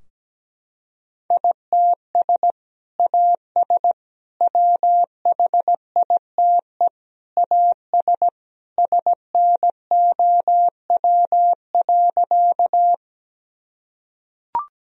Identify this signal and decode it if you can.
Morse code, heard as a keyed tone, about 17 wpm, 700 Hz. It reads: ITS AS WHITE AS SNOW.
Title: Sentences from Top 1000 Words - Review Part 3 - 17wpm